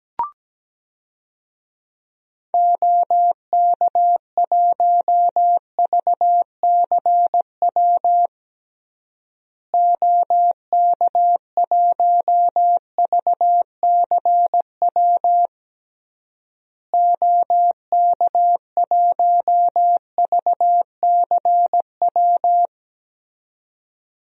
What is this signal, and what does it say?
Morse code, heard as a keyed tone, about 17 wpm, 700 Hz. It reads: OK1VCW OK1VCW OK1VCW
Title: DXCC-100 Call Signs - 6 Characters - Repeated 3x - 17wpm